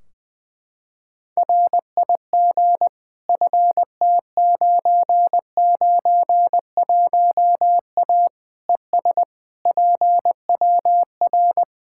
Morse code, heard as a keyed tone, about 20 wpm, 700 Hz. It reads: RIG FT991A ES PWR
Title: Practice QSO - Exchange 2 - Info - 20wpm